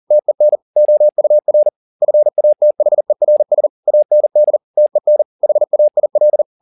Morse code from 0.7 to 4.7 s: OUR FATHERS AND

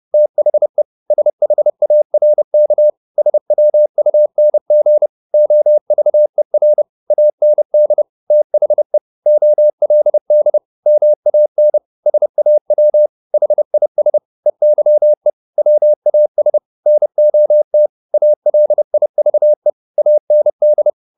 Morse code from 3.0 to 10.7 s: SWUNG OVER AND THE OLD

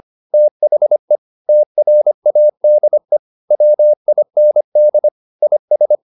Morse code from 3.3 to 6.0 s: WIND IS

